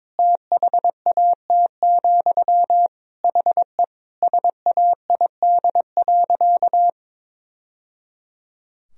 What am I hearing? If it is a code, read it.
Morse code, heard as a keyed tone, about 22 wpm, 700 Hz. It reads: THAT, HE SAID.
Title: Sentences from Top 1000 Words - Review Part 3 - 22wpm